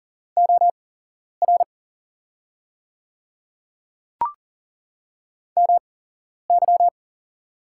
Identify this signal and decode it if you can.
Morse code, heard as a keyed tone, about 40 wpm, spaced as for 11 wpm, 700 Hz. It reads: OR MY
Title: 8x Character Spacing - Top 100 Words - 40wpm